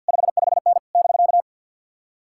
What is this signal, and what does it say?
Morse code, heard as a keyed tone, about 62 wpm, 700 Hz. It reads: 55N <BK>
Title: Speed-Racing Common QSO Phrases - 50wpm